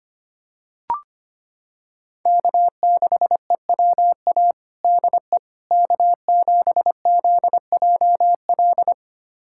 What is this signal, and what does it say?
Morse code, heard as a keyed tone, about 25 wpm, 700 Hz. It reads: K6EWA DE K7ZJL
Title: USA Callsign DE Callsign - 5 Characters - 25wpm